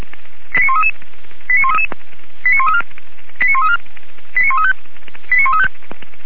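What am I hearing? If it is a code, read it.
Five-tone selective call (ZVEI), heard as quick runs of five tones, tones of about 80 ms. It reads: 89133 89134 89134 89135 89135
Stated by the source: ZVEI five-tone selective call received by SDR.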